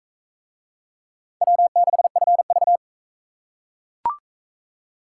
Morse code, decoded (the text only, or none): W6FV